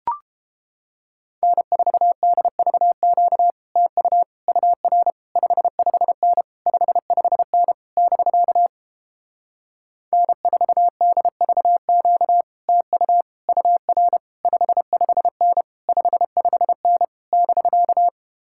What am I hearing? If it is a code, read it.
Morse code, heard as a keyed tone, about 29 wpm, 700 Hz. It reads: N4DVQ TU UR 55N 55N <BK> N4DVQ TU UR 55N 55N <BK>